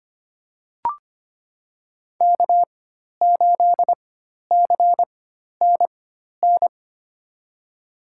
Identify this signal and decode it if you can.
Morse code, heard as a keyed tone, about 25 wpm, spaced as for 12 wpm, 700 Hz. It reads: K8CNN
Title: USA Call Signs - 4x Character Spacing - 25wpm